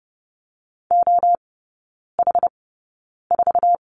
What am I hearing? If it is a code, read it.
Morse code, heard as a keyed tone, about 30 wpm, spaced as for 9 wpm, 700 Hz. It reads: OH4
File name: 081 Lesson 14e - Number 2 - Three Characters 30wpm